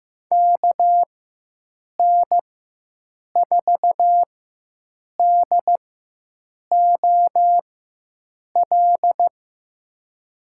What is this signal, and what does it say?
Morse code, heard as a keyed tone, about 15 wpm, spaced as for 7 wpm, 700 Hz. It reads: KN4DOL
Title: USA Call Signs - 6 Characters - 4x Character Spacing - 15wpm